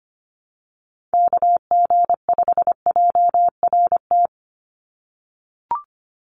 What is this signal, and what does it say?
Morse code, heard as a keyed tone, about 25 wpm, 700 Hz. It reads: KG5JRT
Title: USA Call Signs 25wpm